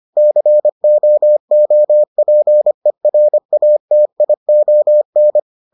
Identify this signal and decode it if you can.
Morse code, heard as a keyed tone, about 25 wpm, 600 Hz. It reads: COOPERATION